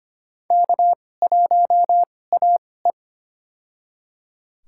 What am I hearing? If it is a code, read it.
Morse code, heard as a keyed tone, about 25 wpm, spaced as for 18 wpm, 700 Hz. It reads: K1AE
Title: USA Call Signs - 4 Characters - 2x Character Spacing - 25wpm